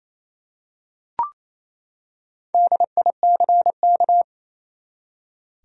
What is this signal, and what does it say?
Morse code, heard as a keyed tone, about 28 wpm, 700 Hz. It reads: DICK
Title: USA Common Names - Max 5 Letters - 28wpm